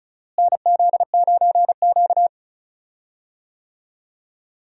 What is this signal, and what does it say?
Morse code, heard as a keyed tone, about 36 wpm, 700 Hz. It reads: NZ9Q
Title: USA Call Signs - 4 Characters - Repeated 3x - 35wpm